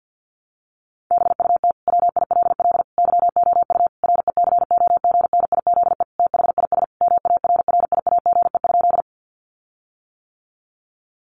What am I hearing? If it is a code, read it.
Morse code, heard as a keyed tone, about 50 wpm, 700 Hz. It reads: BUT WILL YOU RECOGNIZE THIS MARRIAGE?
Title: Sentences from Top 1400 Words 50wpm